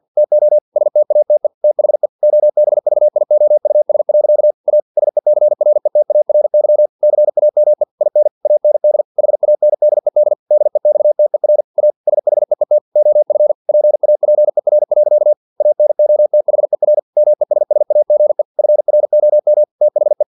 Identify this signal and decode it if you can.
Morse code, heard as a keyed tone, about 50 wpm, 600 Hz. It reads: TO STATE THE OBVIOUS. A SECRETARY CAME IN AND HANDED DEXTER A SHEET OF PAPER. ANOTHER MESSAGE FROM THE
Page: Never 49wpm